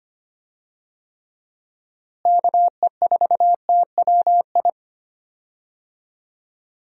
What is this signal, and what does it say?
Morse code, heard as a keyed tone, about 25 wpm, 700 Hz. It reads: KE4TWI
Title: USA Call Signs - 6 Characters - Repeated 3x - 25wpm